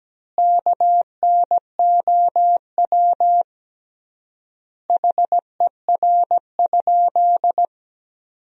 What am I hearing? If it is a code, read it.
Morse code, heard as a keyed tone, about 17 wpm, 700 Hz. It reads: KNOW HER?
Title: 3x Word-Spacing - Sentences from Top 100 Words - 17wpm